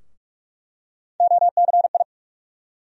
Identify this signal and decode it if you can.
Morse code, heard as a keyed tone, about 46 wpm, 700 Hz. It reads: OCI
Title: USA Call Sign Postfixes - 3 Letters 45wpm